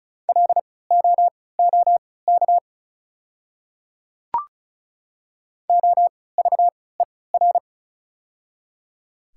Morse code, heard as a keyed tone, about 34 wpm, spaced as for 20 wpm, 690 Hz. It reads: LOOK OVER
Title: 3x Character Spacing - Top 100 Words - 35wpm